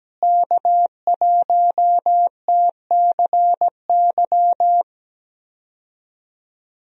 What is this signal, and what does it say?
Morse code, heard as a keyed tone, about 17 wpm, 700 Hz. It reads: K1TCY
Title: USA Call Signs - 5 Characters - Repeated 3x - 17wpm